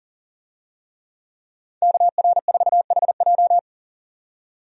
Morse code, heard as a keyed tone, about 40 wpm, 700 Hz. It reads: KR4HJ
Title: USA Call Signs - 5 Characters - 40wpm